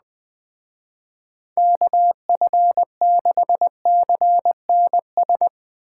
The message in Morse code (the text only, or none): KF6CNS